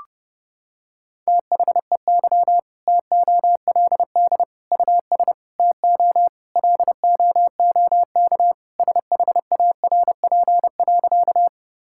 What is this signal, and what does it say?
Morse code, heard as a keyed tone, about 30 wpm, 700 Hz. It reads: THEY TOLD US TO LOOK SHARP.
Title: Sentences from Top 1600 Words 30wpm